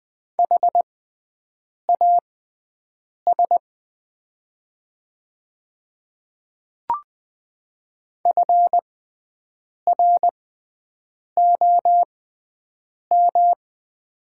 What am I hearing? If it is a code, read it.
Morse code, heard as a keyed tone, about 20 wpm, spaced as for 7 wpm, 700 Hz. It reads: HAS FROM
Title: 6x Character Spacing - Top 200 Words - 20wpm